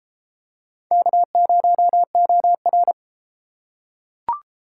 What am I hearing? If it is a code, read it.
Morse code, heard as a keyed tone, about 31 wpm, 700 Hz. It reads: K0OR